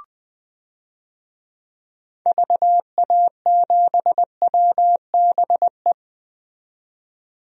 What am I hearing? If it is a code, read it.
Morse code, heard as a keyed tone, about 21 wpm, 700 Hz. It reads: VA7WBE
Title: DXCC-100 Call Signs - 6 Characters - 20wpm